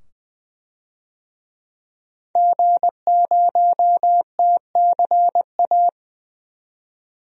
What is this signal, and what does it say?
Morse code, heard as a keyed tone, about 21 wpm, 700 Hz. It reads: G0TCA
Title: DXCC-100 Call Signs - 5 Characters - 20wpm